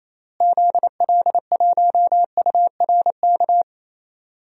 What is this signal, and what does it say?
Morse code, heard as a keyed tone, about 28 wpm, 700 Hz. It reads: ZL1URK
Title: DXCC-100 Call Signs - 6 Characters - Repeated 3x - 28wpm